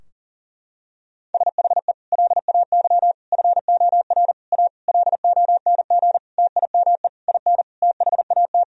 Morse code, heard as a keyed tone, about 40 wpm, 700 Hz. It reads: SHE LAY FOR A LONG TIME IN THAT
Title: Sentences from Top 1000 Words - Review Part 2 - 40wpm